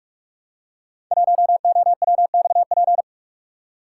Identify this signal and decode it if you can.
Morse code, heard as a keyed tone, about 46 wpm, 700 Hz. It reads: 1OWXP
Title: Code Groups - 5 Letter:Numbers - 45wpm